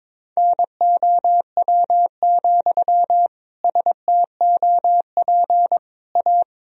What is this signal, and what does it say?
Morse code, heard as a keyed tone, about 22 wpm, 700 Hz. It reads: NOW, STOP A
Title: Sentences from Top 1000 Words - Review Part 1 - 22wpm